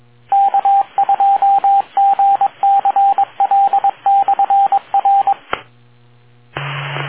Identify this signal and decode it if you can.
Morse code, heard as a keyed tone, about 22 wpm, 790 Hz, with radio noise. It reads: K2GCL/R